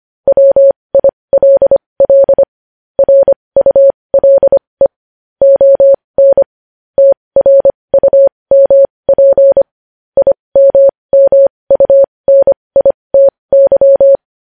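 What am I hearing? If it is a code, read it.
Morse code, heard as a keyed tone, about 25 wpm, spaced as for 20 wpm, 550 Hz. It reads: WILL RULE ON TRUMP IMMUNITY
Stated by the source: Morse code sending news headlines at 20 wpm.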